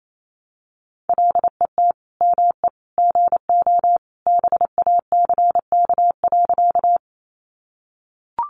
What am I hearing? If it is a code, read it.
Morse code, heard as a keyed tone, about 28 wpm, 700 Hz. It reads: LET ME GO BACK.
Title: Sentences from Top 300 Words 28wpm